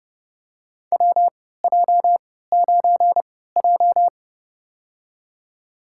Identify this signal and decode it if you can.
Morse code, heard as a keyed tone, about 31 wpm, spaced as for 17 wpm, 700 Hz. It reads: WJ9J
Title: USA Call Signs - 4 Characters - 3x Character Spacing - 30wpm